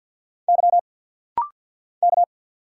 Morse code, heard as a keyed tone, about 50 wpm, 690 Hz. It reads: Y K
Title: Single Letter-Number Alt - Rapid-Fire 50wpm